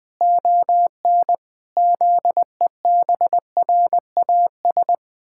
Morse code, heard as a keyed tone, about 20 wpm, 700 Hz. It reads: ON ZEBRAS
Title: Pangram Challenge v2 - 20wpm